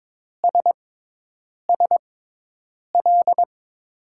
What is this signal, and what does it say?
Morse code, heard as a keyed tone, about 22 wpm, spaced as for 8 wpm, 700 Hz. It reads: SSL
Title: USA Call Signs - 5 Characters - 6x Character Spacing - 22wpm